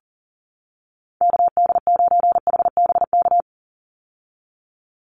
Morse code, heard as a keyed tone, about 40 wpm, 700 Hz. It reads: KD9HBK
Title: USA Call Signs 40wpm